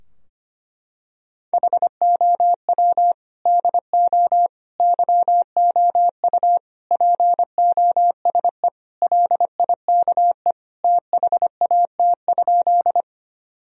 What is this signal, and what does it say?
Morse code, heard as a keyed tone, about 25 wpm, 700 Hz. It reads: HOW DO YOU POSE LIKE THAT?
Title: Sentences from Top 1000 Words - Review Part 2 - 25wpm